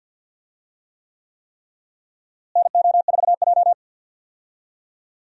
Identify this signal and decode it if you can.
Morse code, heard as a keyed tone, about 52 wpm, 700 Hz. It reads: NO4J
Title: USA Call Signs - 4 Characters - Repeated 3x - 50wpm